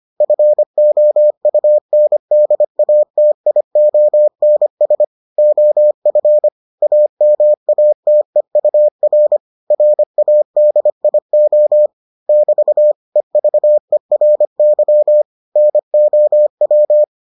Morse code, heard as a keyed tone, about 25 wpm, 600 Hz. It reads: FOUNDATIONS OF AMATEUR RADIO <BT> EVERY NOW